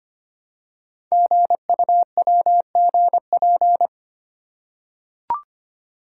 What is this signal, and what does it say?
Morse code, heard as a keyed tone, about 25 wpm, 700 Hz. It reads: GUWGP
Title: Code Groups - 5 Letters - 25wpm